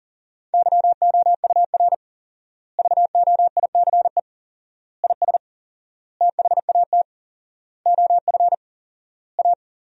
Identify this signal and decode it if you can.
Morse code, heard as a keyed tone, about 40 wpm, 700 Hz. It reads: YOUR VOICE IS THAT OF A
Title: 4x Word-Spacing - Sentences from Top 500 Words - 40wpm